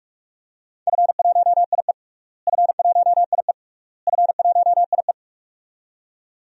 Morse code, heard as a keyed tone, about 47 wpm, 700 Hz. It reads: F1IE F1IE F1IE
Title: DXCC-100 Call Signs - 4 Characters - Repeated 3x - 45wpm